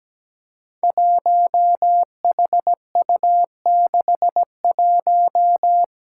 Morse code, heard as a keyed tone, about 17 wpm, 700 Hz. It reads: 1HU61
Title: Five Challenging Letter-Number Combinations 17wpm